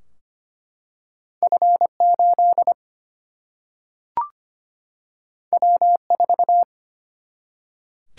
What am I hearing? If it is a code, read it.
Morse code, heard as a keyed tone, about 25 wpm, 700 Hz. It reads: F 8 W 4
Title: Single-Letter Single-Number 25wpm